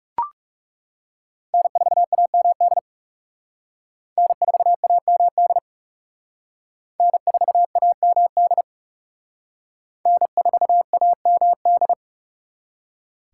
Morse code, heard as a keyed tone, about 36 wpm, 700 Hz. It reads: N4AMD N4AMD N4AMD N4AMD
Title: Speed Racing - USA Call Signs 30wpm